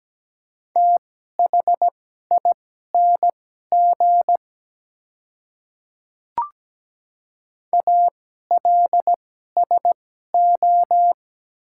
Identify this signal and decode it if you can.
Morse code, heard as a keyed tone, about 17 wpm, spaced as for 12 wpm, 700 Hz. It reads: THING ALSO
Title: 2x Character Spacing - Top 200 Words - 17wpm